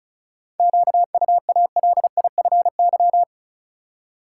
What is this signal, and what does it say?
Morse code, heard as a keyed tone, about 36 wpm, 700 Hz. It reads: QUALIFY
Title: Seven Letter Words - 35wpm